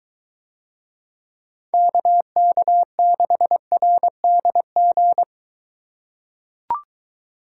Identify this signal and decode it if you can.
Morse code, heard as a keyed tone, about 23 wpm, 700 Hz. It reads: KK6RDG